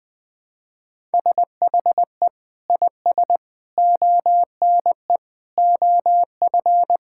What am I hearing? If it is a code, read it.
Morse code, heard as a keyed tone, about 21 wpm, 700 Hz. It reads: SHE IS ONE OF
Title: Sentences with 3-Word Phrases - Spoken Beginning - 20wpm